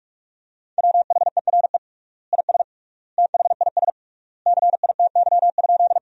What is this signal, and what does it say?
Morse code, heard as a keyed tone, about 46 wpm, 700 Hz. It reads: WHERE IS THIS CITY?
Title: 3x Word-Spacing - Sentences from Top 300 Words - 45wpm